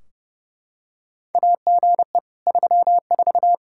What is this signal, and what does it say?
Morse code, heard as a keyed tone, about 32 wpm, 700 Hz. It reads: AGE 34
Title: Practice QSO - Exchange 3 - More Info - 30wpm